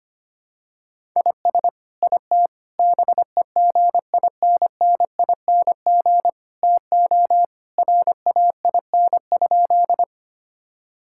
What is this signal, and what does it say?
Morse code, heard as a keyed tone, about 25 wpm, 700 Hz. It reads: IS IT BEGINNING TO RAIN?
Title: Sentences from Top 1000 Words - Review Part 5 - 25wpm